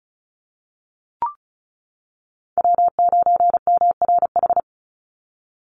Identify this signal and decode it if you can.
Morse code, heard as a keyed tone, about 34 wpm, 700 Hz. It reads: W9MRH
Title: NA SOTA Chaser Callsigns 35wpm